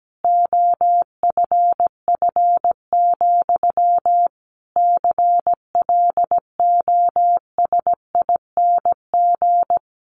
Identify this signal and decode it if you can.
Morse code, heard as a keyed tone, about 17 wpm, 700 Hz. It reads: OFF, CLOSING
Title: A Sound Of Thunder 17wpm